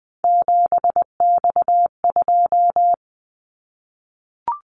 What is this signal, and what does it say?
Morse code, heard as a keyed tone, about 20 wpm, 700 Hz. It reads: 7X2
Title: Callsign Prefixes for Top 100 DXCC - 20wpm